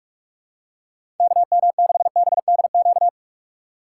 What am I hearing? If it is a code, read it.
Morse code, heard as a keyed tone, about 46 wpm, 700 Hz. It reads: KM6BDQ
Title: USA Call Signs - 6 Characters - 4x Character Spacing - 45wpm